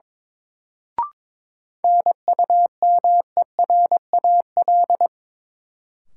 Morse code, heard as a keyed tone, about 22 wpm, 700 Hz. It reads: NUMERAL
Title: Sentences from Top 1000 Words - Review Part 2 - New Word Drill - 22wpm